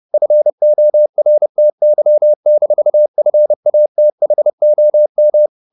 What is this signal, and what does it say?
Morse code, heard as a keyed tone, about 30 wpm, 600 Hz. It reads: FORTY-FATHOM
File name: the old man and the sea / the old man and the sea 1 30wpm